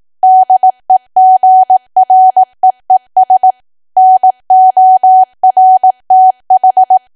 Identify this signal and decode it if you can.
Morse code, heard as a keyed tone, about 18 wpm, 750 Hz. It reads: DEGREES NORTH
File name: cwa / W1AW-Level2 / 18WPM